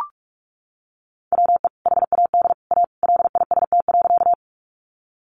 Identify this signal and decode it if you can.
Morse code, heard as a keyed tone, about 44 wpm, 700 Hz. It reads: WE HAD A LIST.
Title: Sentences from Top 400 Words 45wpm